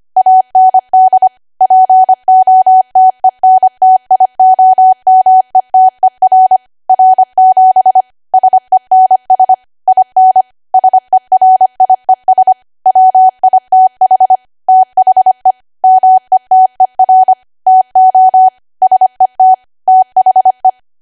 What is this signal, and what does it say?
Morse code, heard as a keyed tone, about 25 wpm, 750 Hz. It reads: AND POTENTIOMETER R7 SENS IN SERIES WITH THE METER TO SET THE
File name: Archive / 25 WPM / 25WPM